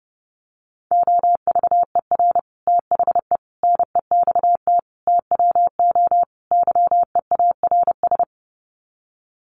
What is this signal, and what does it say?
Morse code, heard as a keyed tone, about 30 wpm, 700 Hz. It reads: OVER THE NEXT TWO YEARS
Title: Sets of 5 Words Encore2 30wpm